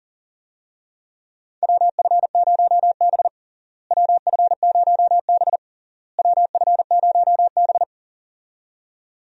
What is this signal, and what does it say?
Morse code, heard as a keyed tone, about 37 wpm, 700 Hz. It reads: WF0B WF0B WF0B
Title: USA Call Signs - 4 Characters - Repeated 3x - 40wpm